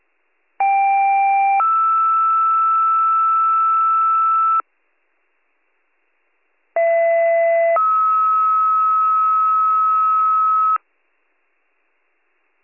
Two-tone page, heard as a one-second tone, then a three-second tone, about 790 then 1320 Hz, about 670 then 1290 Hz. It is heard from 0.6 to 10.8 s.